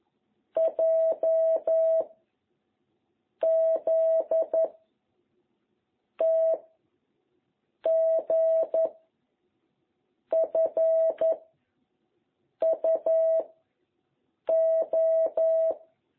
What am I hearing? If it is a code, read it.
Morse code, heard as a keyed tone, about 11 wpm, 660 Hz. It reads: J Z T G F U O